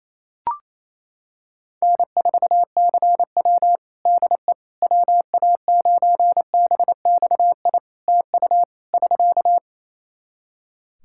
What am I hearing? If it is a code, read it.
Morse code, heard as a keyed tone, about 28 wpm, 700 Hz. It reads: N4CW DE WA9BXI TU <SK>
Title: Practice QSO - Exchange 4 - Ending - 28wpm